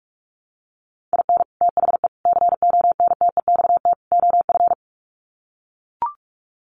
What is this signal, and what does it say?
Morse code, heard as a keyed tone, about 44 wpm, 700 Hz. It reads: IN THE CONTEXT OF
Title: Sets of 4 Words 45wpm